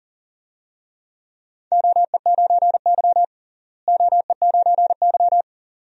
Morse code, heard as a keyed tone, about 39 wpm, 700 Hz. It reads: OE9Y OE9Y
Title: DXCC-100 Call Signs - 4 Characters - Repeated 2x - 40wpm